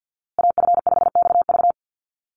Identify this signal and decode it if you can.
Morse code, heard as a keyed tone, about 50 wpm, 700 Hz. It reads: AF5XV
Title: USA Call Signs Repeated 2x 50wpm